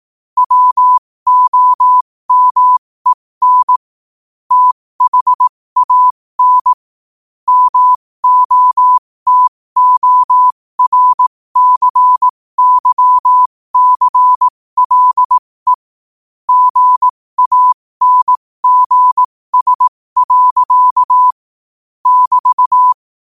Morse code, heard as a keyed tone, about 18 wpm, spaced as for 15 wpm, 1000 Hz. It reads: WOMEN THAN MOTORCYCLE GANGS. <BT>
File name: qotdfiles / 15wpm